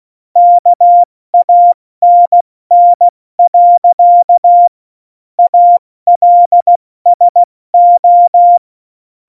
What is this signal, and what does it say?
Morse code, heard as a keyed tone, about 16 wpm, 700 Hz. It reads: KANN. ALSO
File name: Morsekurs / Witze / Witz 26 Mai, 2020 70 F80